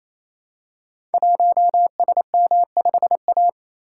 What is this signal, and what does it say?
Morse code, heard as a keyed tone, about 28 wpm, 700 Hz. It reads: 1SM5A
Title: Code Groups - 5 Letter:Numbers - 28wpm